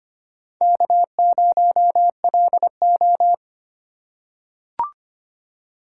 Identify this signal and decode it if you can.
Morse code, heard as a keyed tone, about 25 wpm, 700 Hz. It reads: K0LO
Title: USA Call Signs - 4 Characters - 3x Character Spacing - 25wpm